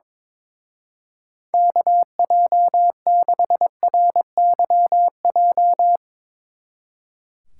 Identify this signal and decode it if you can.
Morse code, heard as a keyed tone, about 22 wpm, 700 Hz. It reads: KJ6RYJ
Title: Speed Racing - USA 2x3 Call Signs 22wpm